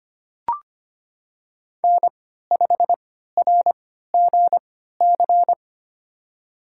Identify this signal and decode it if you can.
Morse code, heard as a keyed tone, about 25 wpm, spaced as for 14 wpm, 700 Hz. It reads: N5RGC
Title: USA Call Signs - 5 Characters - 3x Character Spacing - 25wpm